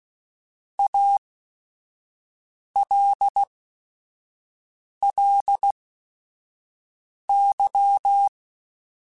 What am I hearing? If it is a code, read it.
Morse code, heard as a keyed tone, about 16 wpm, spaced as for 5 wpm, 800 Hz. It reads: ALLY
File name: Technical Subjects / Technical Subjects-mp3s / AND Gate @5